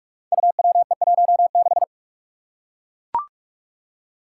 Morse code, heard as a keyed tone, about 41 wpm, 700 Hz. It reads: UWE16
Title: Five Challenging Letter-Number Combinations 45wpm